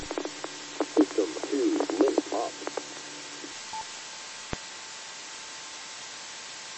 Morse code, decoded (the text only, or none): none